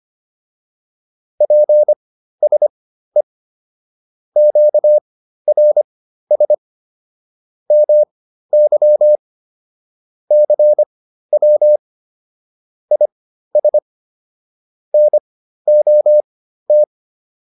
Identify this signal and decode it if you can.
Morse code, heard as a keyed tone, about 25 wpm, spaced as for 13 wpm, 600 Hz. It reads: PSE QRS MY CW IS NOT
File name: QSO 210 13